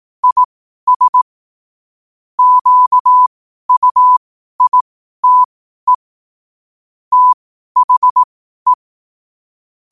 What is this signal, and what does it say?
Morse code, heard as a keyed tone, about 18 wpm, spaced as for 12 wpm, 1000 Hz. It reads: IS QUITE THE
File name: qotdfiles / 12wpm